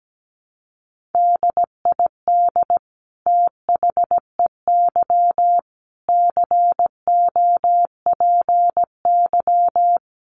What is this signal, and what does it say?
Morse code, heard as a keyed tone, about 17 wpm, 700 Hz. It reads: DID THEY COPY
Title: Sentences from Top 1400 Words 17wpm